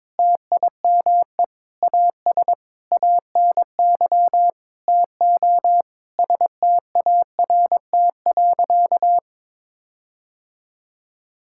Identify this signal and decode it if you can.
Morse code, heard as a keyed tone, about 21 wpm, 700 Hz. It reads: TIME AS ANY TO START.
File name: Sentences with 7-Word Phrases - 22wpm